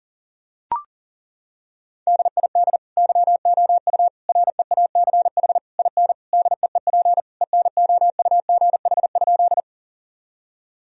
Morse code, heard as a keyed tone, about 37 wpm, 700 Hz. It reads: DID YOU REACH IN DEEP ENOUGH?
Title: Sentences from Top 1000 Words - Review Part 3 - 40wpm